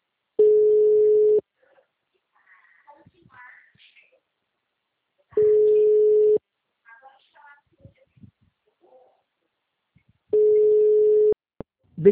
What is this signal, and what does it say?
Ringback tone (Europe, 425 Hz), with speech.